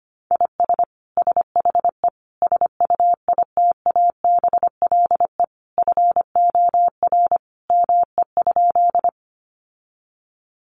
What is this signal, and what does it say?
Morse code, heard as a keyed tone, about 25 wpm, 700 Hz. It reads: IS SHE SUITABLE FOR ME?
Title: Sentences from Top 1400 Words 25wpm